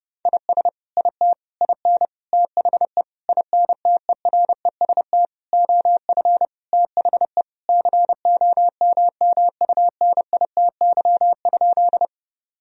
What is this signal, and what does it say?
Morse code, heard as a keyed tone, about 30 wpm, 700 Hz. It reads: IS IT IN THE INTEREST OF THE COMMUNITY?
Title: Sentences from Top 1000 Words - Review Part 4 - 30wpm